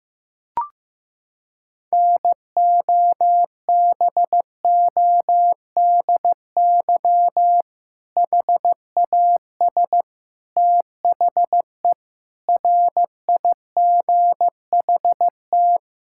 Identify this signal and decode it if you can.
Morse code, heard as a keyed tone, about 16 wpm, 700 Hz. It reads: NOBODY HAS THE RIGHT
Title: Sentences from Top 1600 Words 15wpm